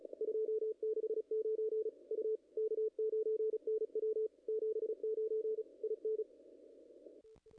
Morse code, heard as a keyed tone, about 31 wpm, 440 Hz. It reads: 269 U K9NW 79 IN
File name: contests / ss-cw-k5tr-op-n5rz -0000